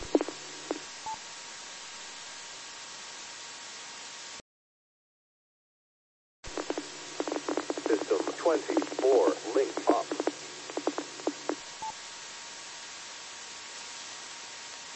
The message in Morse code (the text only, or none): none